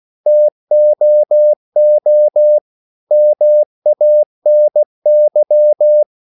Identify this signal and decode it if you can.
Morse code, heard as a keyed tone, about 16 wpm, 600 Hz. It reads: TOO MANY
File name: CW / Practice / EnglishSayings16